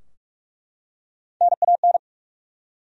Morse code, heard as a keyed tone, about 48 wpm, 700 Hz. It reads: NAN
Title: USA Common Names - Max 3 Letters - 45wpm